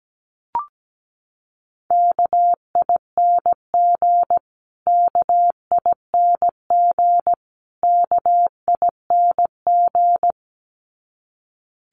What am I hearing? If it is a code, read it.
Morse code, heard as a keyed tone, about 17 wpm, 700 Hz. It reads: KING KING KING
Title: Top 500 Words Repeated 17wpm